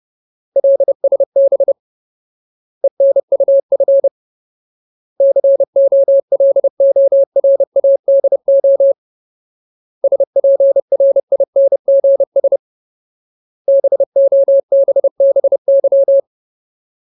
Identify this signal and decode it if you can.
Morse code, heard as a keyed tone, about 30 wpm, 550 Hz. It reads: LSB ENUF COLORADO SPRINGS BOBBY
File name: Ultimate Word List / 30WPM Ultimate